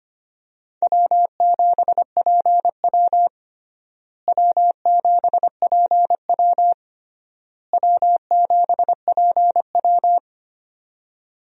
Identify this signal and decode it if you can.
Morse code, heard as a keyed tone, about 25 wpm, 700 Hz. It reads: W7PW W7PW W7PW
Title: USA Call Signs - 4 Characters - Repeated 3x - 25wpm